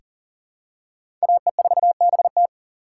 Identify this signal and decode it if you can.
Morse code, heard as a keyed tone, about 40 wpm, 690 Hz. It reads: AE4BT
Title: USA Call Signs - 5 Characters - 3x Character Spacing - 40wpm